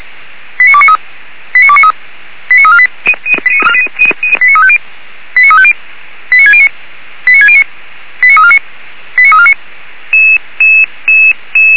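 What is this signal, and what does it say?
Five-tone selective call (ZVEI), heard as quick runs of five tones, tones of about 70 ms. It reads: 89293 89293 89348 89350 89350 89609 89609 89349 89349
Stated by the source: ZVEI five-tone alert received by SDR, fire brigade feuerwehr-orlamuende.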